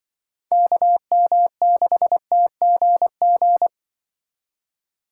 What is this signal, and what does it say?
Morse code, heard as a keyed tone, about 24 wpm, 700 Hz. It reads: KM6TGG
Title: Speed Racing - USA 2x3 Call Signs 20wpm